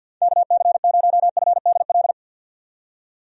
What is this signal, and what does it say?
Morse code, heard as a keyed tone, about 53 wpm, 700 Hz. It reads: KC0FDL
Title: USA Call Signs - 6 Characters - 50wpm